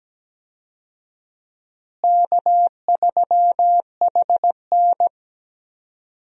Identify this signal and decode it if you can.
Morse code, heard as a keyed tone, about 17 wpm, 700 Hz. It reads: K3HN
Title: USA Call Signs - 4 Characters - Repeated 3x - 17wpm